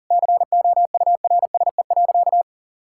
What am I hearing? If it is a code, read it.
Morse code, heard as a keyed tone, about 40 wpm, 700 Hz. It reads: COURSE.
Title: Sentences from Top 1000 Words - Review Part 5 - 40wpm